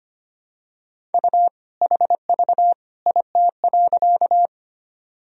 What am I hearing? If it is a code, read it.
Morse code, heard as a keyed tone, about 25 wpm, 700 Hz. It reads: U HV IT.
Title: Sentences from Top 100 Words and Abbreviations 25wpm